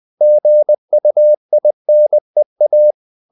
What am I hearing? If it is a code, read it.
Morse code, heard as a keyed tone, about 20 wpm, 600 Hz. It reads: GUINEA
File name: CW / Practice / NationalCapitals20